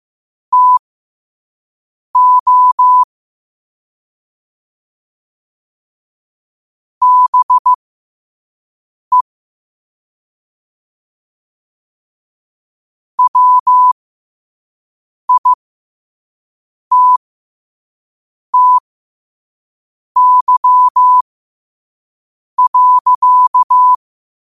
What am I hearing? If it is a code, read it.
Morse code, heard as a keyed tone, about 15 wpm, spaced as for 5 wpm, 1000 Hz. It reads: TO BE WITTY.
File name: qotdfiles / 5wpm